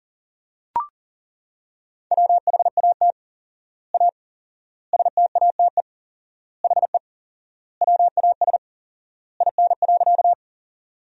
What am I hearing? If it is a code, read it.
Morse code, heard as a keyed tone, about 40 wpm, 700 Hz. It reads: WHAT A STATE HE WAS IN.
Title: 4x Word-Spacing - Sentences from Top 400 Words - 40wpm